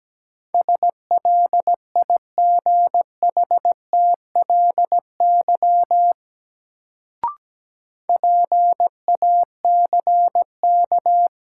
Morse code, heard as a keyed tone, about 17 wpm, 700 Hz. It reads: SLIGHTLY PACK
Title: Sentences from Top 1300 Words - New Word Drill - 17wpm